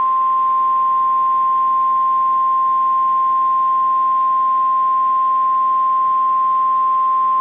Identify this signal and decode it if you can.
Emergency Alert System (SAME) tones (the 1050 Hz attention tone). It is heard from 0.9 to 6.8 s.
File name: EAS Logs / KMNR